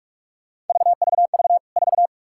Morse code, heard as a keyed tone, about 47 wpm, 700 Hz. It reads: VVV 4
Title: K1USN SST QSO Practice 45wpm